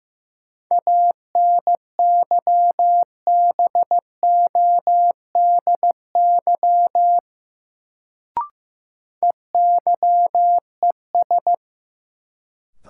Morse code, heard as a keyed tone, about 15 wpm, 700 Hz. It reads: ANYBODY EYES
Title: Sets of 5 Words - New Word Drill - 15wpm